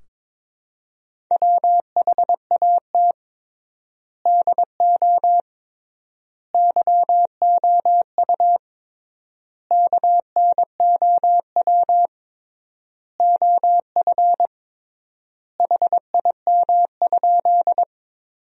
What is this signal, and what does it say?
Morse code, heard as a keyed tone, about 22 wpm, 700 Hz. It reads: WHAT DO YOU KNOW OF HIM?
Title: 3x Word-Spacing - Sentences from Top 100 Words - 22wpm